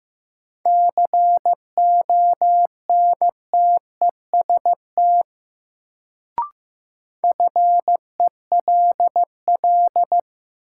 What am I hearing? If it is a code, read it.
Morse code, heard as a keyed tone, about 15 wpm, 700 Hz. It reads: CONTEST FELL
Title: Sentences from Top 1600 Words - New Word Drill Alt - 15wpm